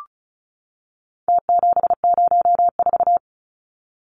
Morse code, heard as a keyed tone, about 34 wpm, 700 Hz. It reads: T704
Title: Numbers - Contest Style - 4-Digit 35wpm